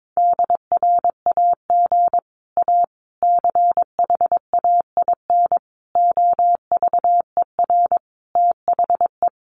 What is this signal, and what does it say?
Morse code, heard as a keyed tone, about 22 wpm, 700 Hz. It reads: DRAG A CHAIN OVER THE
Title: Sentences from Top 1500 Words 22wpm